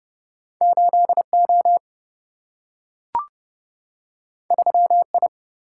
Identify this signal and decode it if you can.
Morse code, heard as a keyed tone, about 30 wpm, 700 Hz. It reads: 8O 3I
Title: Single-Letter Single-Number 30wpm